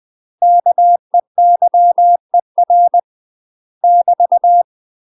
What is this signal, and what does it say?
Morse code, heard as a keyed tone, about 20 wpm, 700 Hz. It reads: KEYER <BT>
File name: UK MASTER QSOs TRAINING / 20WPM / UK MASTER QSOs 0023